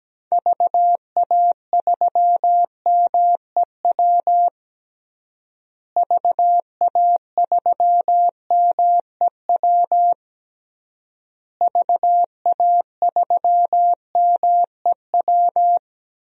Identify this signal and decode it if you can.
Morse code, heard as a keyed tone, about 17 wpm, 700 Hz. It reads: VA3MEW VA3MEW VA3MEW
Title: DXCC-100 Call Signs - 6 Characters - Repeated 3x - 17wpm